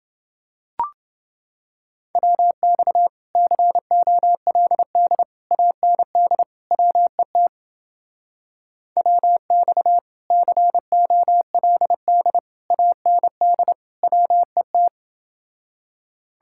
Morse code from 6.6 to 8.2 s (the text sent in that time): WET